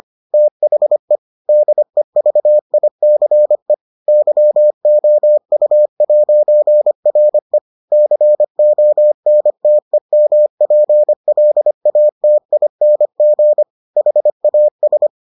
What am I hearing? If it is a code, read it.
Morse code, heard as a keyed tone, about 25 wpm, 600 Hz. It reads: THE DEVICE YOU'RE CONTEMPLATING HAS